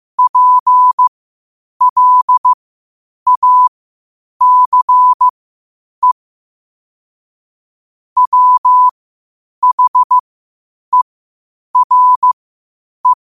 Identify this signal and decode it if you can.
Morse code, heard as a keyed tone, about 15 wpm, spaced as for 8 wpm, 1000 Hz. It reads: PLACE WHERE
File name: qotdfiles / 8wpm